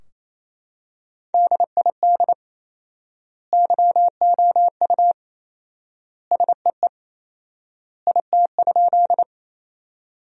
Morse code, heard as a keyed tone, about 27 wpm, 700 Hz. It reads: DID YOU SEE IT?
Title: 4x Word-Spacing - Sentences from Top 100 Words - 28wpm